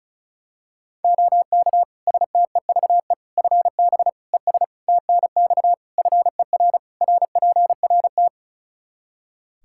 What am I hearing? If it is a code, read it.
Morse code, heard as a keyed tone, about 34 wpm, 700 Hz. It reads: OK STEVE FB ES TNX FER RPRT
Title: Practice QSO - Exchange 2 - Info - 35wpm